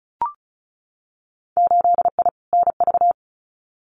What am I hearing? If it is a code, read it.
Morse code, heard as a keyed tone, about 34 wpm, 700 Hz. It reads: 8I NV
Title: Winter Field Day Exchanges Encore 35wpm